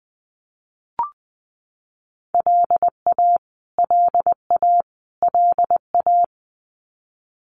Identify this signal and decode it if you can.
Morse code, heard as a keyed tone, about 20 wpm, 700 Hz. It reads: LA LA LA
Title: US States Repeated 3x 20wpm